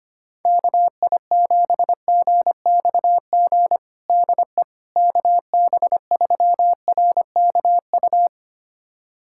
Morse code, heard as a keyed tone, about 25 wpm, 700 Hz. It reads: KI7GXG DE KB3RKU
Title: USA Callsign DE Callsign 25wpm